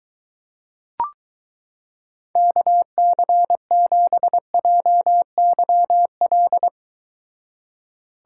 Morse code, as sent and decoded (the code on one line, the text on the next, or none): -.- -.-. --... .--- -.-- .-..
KC7JYL